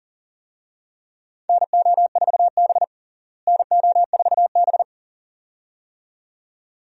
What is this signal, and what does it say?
Morse code, heard as a keyed tone, about 37 wpm, 700 Hz. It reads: NO4B NO4B
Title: USA Call Signs - 4 Characters - Repeated 2x - 40wpm